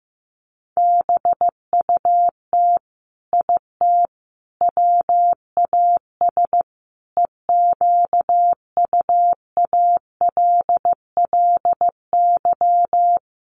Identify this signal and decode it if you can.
Morse code, heard as a keyed tone, about 15 wpm, 700 Hz. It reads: BUT IT WAS EQUALLY
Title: Sentences from Top 1400 Words 15wpm